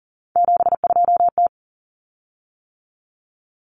Morse code, heard as a keyed tone, about 40 wpm, 700 Hz. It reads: 72T
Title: Common Cut Numbers - 3-Digit 40wpm